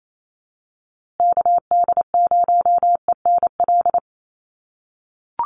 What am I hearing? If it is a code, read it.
Morse code, heard as a keyed tone, about 28 wpm, 700 Hz. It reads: KD0ENL
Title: USA Call Signs Repeated 3x 28wpm